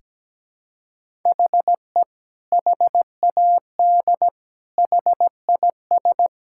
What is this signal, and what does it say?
Morse code, heard as a keyed tone, about 17 wpm, 700 Hz. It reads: HE HAD HIS
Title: Sentences from Top 1000 Words - Review Part 5 - 17wpm